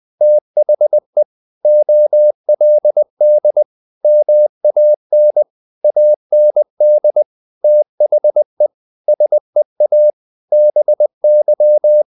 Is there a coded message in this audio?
Yes: Morse code from 0.2 to 12.0 s, heard as a keyed tone, about 20 wpm, 600 Hz. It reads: THE OLD MAN AND THE SEA BY